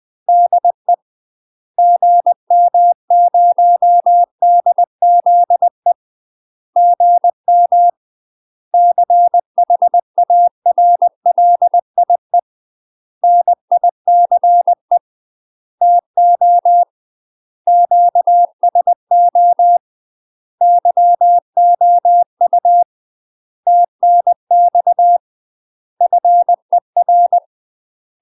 Morse code, heard as a keyed tone, about 20 wpm, 700 Hz. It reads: DE GM0DZE GM CHARLIE NICE TO QSO YOU TNX FER
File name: UK MASTER QSOs TRAINING / 20WPM / UK MASTER QSOs 0067